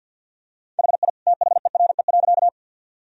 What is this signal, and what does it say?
Morse code, heard as a keyed tone, about 50 wpm, 700 Hz. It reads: HI THERE.
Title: Sentences from Top 1000 Words - Review Part 3 - 50wpm